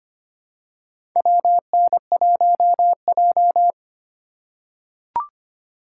Morse code, heard as a keyed tone, about 25 wpm, 700 Hz. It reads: WN1J
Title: USA Call Signs - 4 Characters - 2x Character Spacing - 25wpm